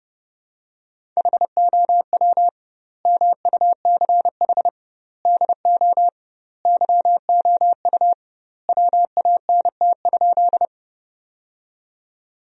Morse code, heard as a keyed tone, about 30 wpm, 700 Hz. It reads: HOW MUCH DO YOU WANT?
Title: 2x Word-Spacing - Sentences from Top 200 Words - 30wpm